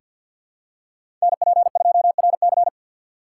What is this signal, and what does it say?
Morse code, heard as a keyed tone, about 47 wpm, 700 Hz. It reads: NP2RC
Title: USA Call Signs - 5 Characters - 4x Character Spacing - 50wpm